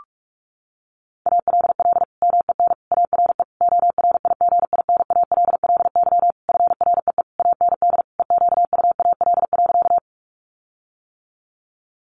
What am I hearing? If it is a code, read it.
Morse code, heard as a keyed tone, about 44 wpm, 700 Hz. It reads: ALL MEN ARE ORIGINALLY FREE AND EQUAL.
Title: Sentences from Top 1500 Words 45wpm